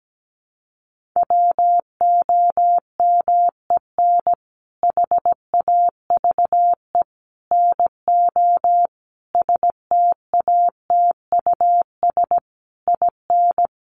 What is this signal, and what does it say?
Morse code, heard as a keyed tone, about 17 wpm, 700 Hz. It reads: WOMEN HAVE NO STATUS IN
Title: Sentences from Top 1400 Words 17wpm